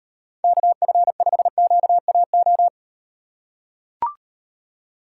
Morse code, heard as a keyed tone, about 38 wpm, 700 Hz. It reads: KF5QAO